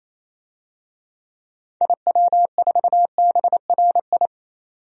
Morse code, heard as a keyed tone, about 28 wpm, 700 Hz. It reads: IW4BRI
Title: DXCC-100 Call Signs - 6 Characters - 28wpm